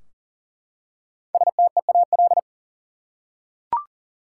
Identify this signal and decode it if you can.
Morse code, heard as a keyed tone, about 37 wpm, 700 Hz. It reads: STEAL
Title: Five Letter Words - 40wpm